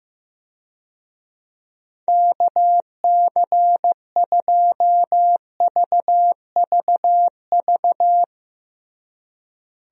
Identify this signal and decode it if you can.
Morse code, heard as a keyed tone, about 16 wpm, 700 Hz. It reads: KC2VVV
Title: USA Call Signs - 6 Characters - Repeated 2x - 15wpm